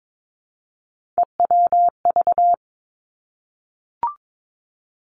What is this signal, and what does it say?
Morse code, heard as a keyed tone, about 22 wpm, 700 Hz. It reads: EWV
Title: Three Challenging Letter-Number Combinations 22wpm